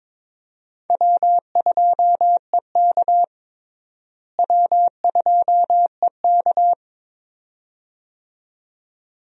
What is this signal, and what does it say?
Morse code, heard as a keyed tone, about 22 wpm, 700 Hz. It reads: W2EK W2EK
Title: USA Call Signs - 4 Characters - Repeated 2x - 22wpm